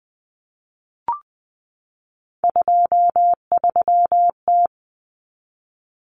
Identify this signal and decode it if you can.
Morse code, heard as a keyed tone, about 20 wpm, 700 Hz. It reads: 23T
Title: Common Cut Numbers - 3-Digit 20wpm